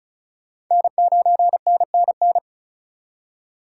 Morse code, heard as a keyed tone, about 46 wpm, 700 Hz. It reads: N9NNN